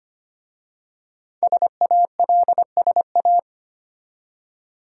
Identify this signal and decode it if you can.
Morse code, heard as a keyed tone, about 25 wpm, 700 Hz. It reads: SALSA
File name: practice / 25wpm / Easy Words with Hints - 25wpm